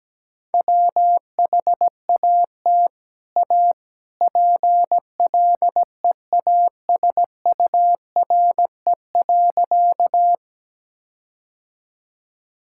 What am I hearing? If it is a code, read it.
Morse code, heard as a keyed tone, about 17 wpm, 700 Hz. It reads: WHAT A PLEASURE.
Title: Sentences from Top 1600 Words 17wpm